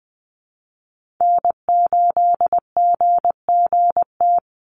Morse code, heard as a keyed tone, about 20 wpm, 700 Hz. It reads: N8GGT
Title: USA Call Signs 20wpm